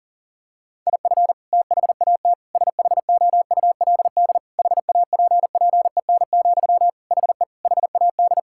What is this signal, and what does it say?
Morse code, heard as a keyed tone, about 40 wpm, 700 Hz. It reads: IF THAT SHOULD HAPPEN, HE HAD
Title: Sentences from Top 1000 Words - Review Part 6 - 40wpm